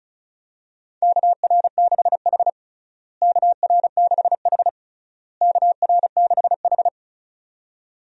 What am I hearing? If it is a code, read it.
Morse code, heard as a keyed tone, about 34 wpm, 700 Hz. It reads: KR6H KR6H KR6H
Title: USA Call Signs - 4 Characters - Repeated 3x - 35wpm